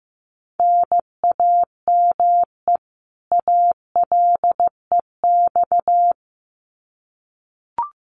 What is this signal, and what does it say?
Morse code, heard as a keyed tone, about 15 wpm, 700 Hz. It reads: NAME ALEX
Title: Common QSO Phrases 15wpm